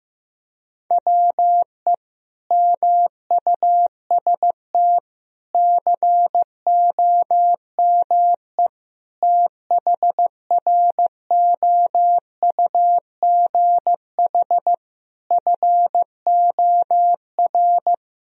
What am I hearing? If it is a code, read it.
Morse code, heard as a keyed tone, about 15 wpm, 700 Hz. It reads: WE MUST COME THROUGH FOR
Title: Sentences from Top 1000 Words - Review Part 1 - 15wpm